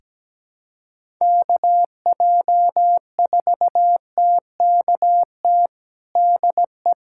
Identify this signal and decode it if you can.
Morse code, heard as a keyed tone, about 17 wpm, 700 Hz. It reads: KJ4TKT DE
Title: USA Callsign DE Callsign 17wpm